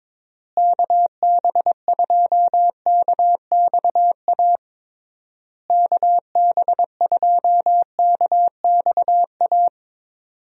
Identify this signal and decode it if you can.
Morse code, heard as a keyed tone, about 22 wpm, 700 Hz. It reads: KB2KXA KB2KXA
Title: USA Call Signs - 6 Characters - Repeated 3x - 22wpm